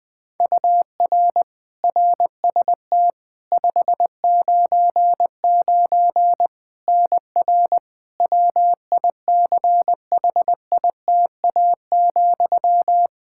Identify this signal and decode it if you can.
Morse code, heard as a keyed tone, about 20 wpm, 700 Hz. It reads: UR RST 599 NR WICHITA,
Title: K1USN SST QSO Practice 20wpm